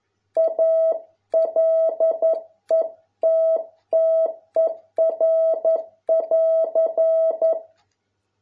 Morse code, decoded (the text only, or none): A LETTER<AR>